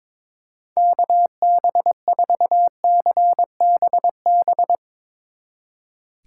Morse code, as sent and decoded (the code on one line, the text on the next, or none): -.- -... ....- -.-. -... -...
KB4CBB